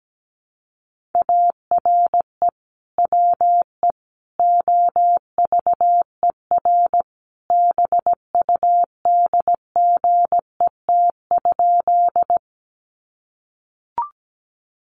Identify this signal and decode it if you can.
Morse code, heard as a keyed tone, about 17 wpm, 700 Hz. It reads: ARE WE OVER BUDGET?
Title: Sentences from Top 1200 Words 17wpm